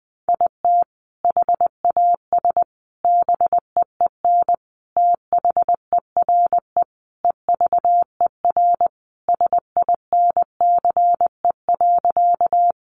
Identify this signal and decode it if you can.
Morse code, heard as a keyed tone, about 20 wpm, 700 Hz. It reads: IT HAS BEEN THERE EVER SINCE.
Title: Sentences from Top 400 Words 20wpm